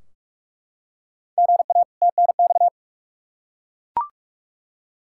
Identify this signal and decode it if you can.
Morse code, heard as a keyed tone, about 41 wpm, 700 Hz. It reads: GA TNX
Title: Practice QSO - Exchange 1 - Beginning - 45wpm